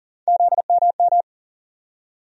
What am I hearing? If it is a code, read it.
Morse code, heard as a keyed tone, about 40 wpm, 700 Hz. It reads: ZMM
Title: USA Call Sign Postfixes - 3 Letters 40wpm